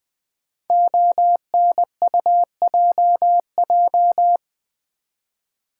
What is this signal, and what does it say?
Morse code, heard as a keyed tone, about 20 wpm, 700 Hz. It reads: ONUJJ
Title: Code Groups - 5 Letters - 20wpm